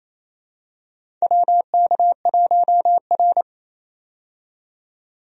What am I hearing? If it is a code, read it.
Morse code, heard as a keyed tone, about 27 wpm, 700 Hz. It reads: WK1R
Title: USA Call Signs - 4 Characters - 28wpm